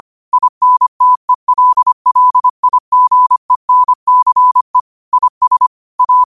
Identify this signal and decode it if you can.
Morse code, heard as a keyed tone, about 25 wpm, 1000 Hz. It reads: INTELLIGENCE IS A